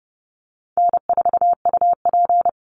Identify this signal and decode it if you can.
Morse code, heard as a keyed tone, about 30 wpm, 700 Hz. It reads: N4UP
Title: Call Signs Repeated 3x - 30wpm